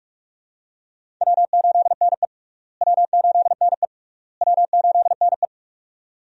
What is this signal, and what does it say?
Morse code, heard as a keyed tone, about 46 wpm, 700 Hz. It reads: W8NE W8NE W8NE
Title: USA Call Signs - 4 Characters - Repeated 3x - 45wpm